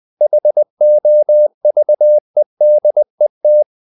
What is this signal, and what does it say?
Morse code, heard as a keyed tone, about 20 wpm, 600 Hz. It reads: HOVEDET
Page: Ferie 20 wpm